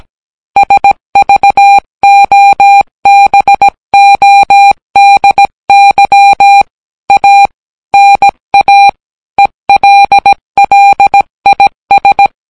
Morse code, heard as a keyed tone, about 17 wpm, 800 Hz. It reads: SVOBODY A NA ELLIS